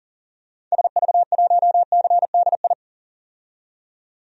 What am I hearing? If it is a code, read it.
Morse code, heard as a keyed tone, about 40 wpm, 700 Hz. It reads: SV1CDI
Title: DXCC-100 Call Signs - 6 Characters - 40wpm